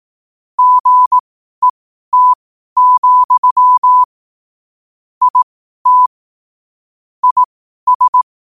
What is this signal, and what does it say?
Morse code, heard as a keyed tone, about 18 wpm, spaced as for 12 wpm, 1000 Hz. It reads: GET, IT IS
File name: qotdfiles / 12wpm